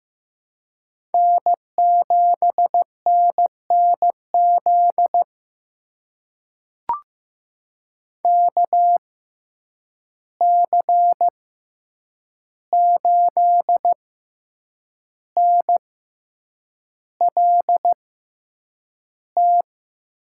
Morse code, heard as a keyed tone, about 15 wpm, spaced as for 7 wpm, 700 Hz. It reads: N7NNZ KC8NLT
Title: USA Call Signs - 6x Character Spacing - 15wpm